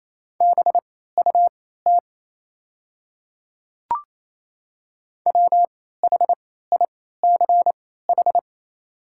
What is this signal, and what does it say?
Morse code, heard as a keyed tone, about 27 wpm, spaced as for 16 wpm, 690 Hz. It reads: BUT WHICH
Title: 3x Character Spacing - Top 100 Words - 28wpm